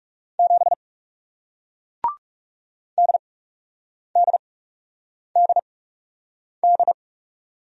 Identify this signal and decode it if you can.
Morse code, heard as a keyed tone, about 40 wpm, spaced as for 8 wpm, 700 Hz. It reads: 7 DDDD